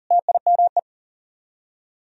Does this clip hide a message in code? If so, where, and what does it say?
Yes: Morse code from 0.1 to 0.8 s, heard as a keyed tone, about 50 wpm, 700 Hz. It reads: TIME